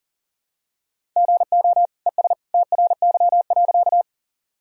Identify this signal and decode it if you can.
Morse code, heard as a keyed tone, about 40 wpm, 700 Hz. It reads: GO ES TRY.
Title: Sentences from Top 200 Words and Abbreviations 40wpm